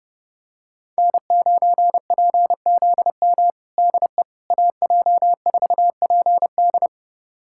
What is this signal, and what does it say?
Morse code, heard as a keyed tone, about 30 wpm, 700 Hz. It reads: N9PZM DE AJ4PD
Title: USA Callsign DE Callsign - 5 Characters - 30wpm